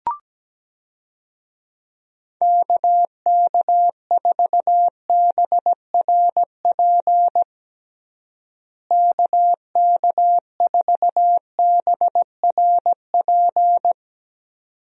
Morse code, heard as a keyed tone, about 17 wpm, 700 Hz. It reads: KK4BRP KK4BRP
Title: USA Call Signs - 6 Characters - Repeated 2x - 17wpm